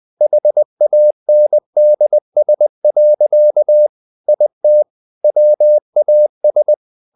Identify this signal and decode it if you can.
Morse code, heard as a keyed tone, about 20 wpm, 600 Hz. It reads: HANDS. IT WAS
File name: the old man and the sea part 2 20wpm